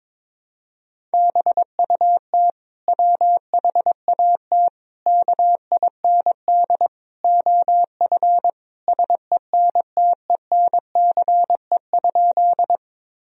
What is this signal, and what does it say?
Morse code, heard as a keyed tone, about 22 wpm, 700 Hz. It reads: BUT WHAT KIND OF SENTENCE?
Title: Sentences from Top 1000 Words - Review Part 1 - 22wpm